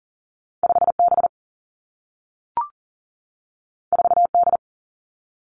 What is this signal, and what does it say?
Morse code, heard as a keyed tone, about 42 wpm, 700 Hz. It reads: 5 B 4 D
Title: Two Challenging Letter-Number Combinations 40wpm